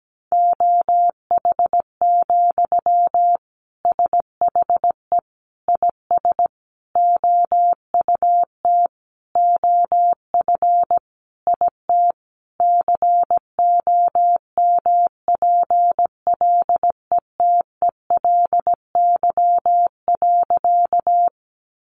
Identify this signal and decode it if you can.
Morse code, heard as a keyed tone, about 17 wpm, 700 Hz. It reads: OH, SHE IS OUT OF IT COMPLETELY.
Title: Sentences from Top 1100 Words 17wpm